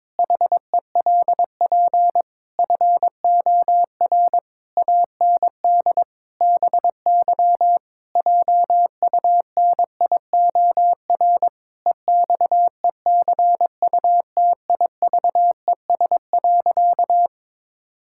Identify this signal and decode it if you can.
Morse code, heard as a keyed tone, about 22 wpm, 700 Hz. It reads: HELP FOR AND BY JUNIOR EXECUTIVES.
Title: Pangram Challenge v2 - 22wpm